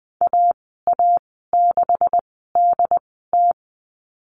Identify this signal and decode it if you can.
Morse code, heard as a keyed tone, about 20 wpm, spaced as for 15 wpm, 700 Hz. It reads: AA6DT
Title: 147 Lesson 24g - Letter y - Call Signs 20wpm